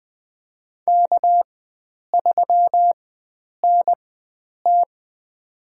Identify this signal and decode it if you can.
Morse code, heard as a keyed tone, about 20 wpm, spaced as for 9 wpm, 700 Hz. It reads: K3NT
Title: USA Call Signs - 4 Characters - 4x Character Spacing - 20wpm